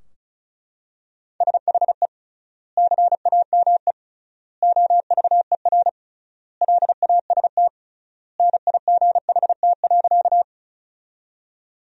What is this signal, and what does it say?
Morse code, heard as a keyed tone, about 34 wpm, 700 Hz. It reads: SHE CAME OVER LAST NIGHT.
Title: 3x Word-Spacing - Sentences from Top 300 Words - 35wpm